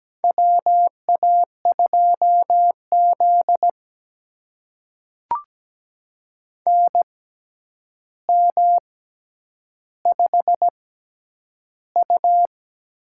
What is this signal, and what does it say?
Morse code, heard as a keyed tone, about 17 wpm, spaced as for 9 wpm, 700 Hz. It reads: WA2Z NM5U
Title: USA Call Signs - 4 Characters - 6x Character Spacing - 17wpm